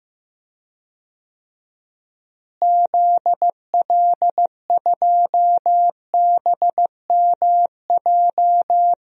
Morse code, heard as a keyed tone, about 15 wpm, 700 Hz. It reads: ZL2BMJ
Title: DXCC-100 Call Signs - 6 Characters - Repeated 3x - 15wpm